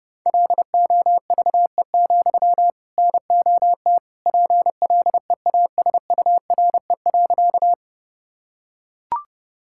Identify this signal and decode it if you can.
Morse code, heard as a keyed tone, about 32 wpm, 700 Hz. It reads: LOVE, NOT PLEASURE.
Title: Sentences from Top 1600 Words 30wpm